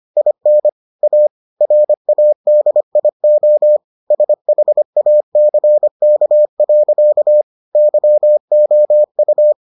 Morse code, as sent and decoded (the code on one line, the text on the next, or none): .. -. / .- / .-. .- -.. .. --- / ... .... .- -.-. -.- .-.-.- / -.-- --- ..-
IN A RADIO SHACK. YOU